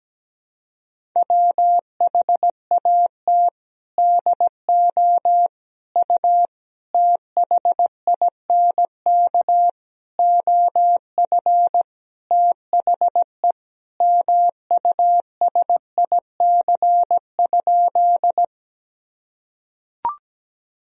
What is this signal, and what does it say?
Morse code, heard as a keyed tone, about 17 wpm, 700 Hz. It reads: WHAT DO U THINK OF THE MUSIC?
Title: Sentences from Top 300 Words and Abbreviations 17wpm